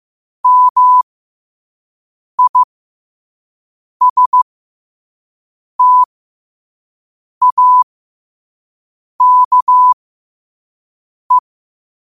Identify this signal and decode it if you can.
Morse code, heard as a keyed tone, about 15 wpm, spaced as for 5 wpm, 1000 Hz. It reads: MISTAKE
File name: qotdfiles / 5wpm